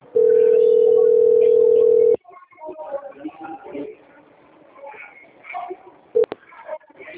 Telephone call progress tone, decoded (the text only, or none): ringback tone